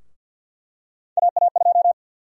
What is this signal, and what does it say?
Morse code, heard as a keyed tone, about 53 wpm, 700 Hz. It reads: AA2
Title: USA Call Sign Prefixes - Repeated 2x - 50wpm